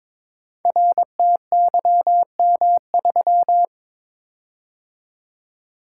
Morse code, heard as a keyed tone, about 22 wpm, 700 Hz. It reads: RTYM3
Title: Code Groups - 5 Letter:Numbers - 22wpm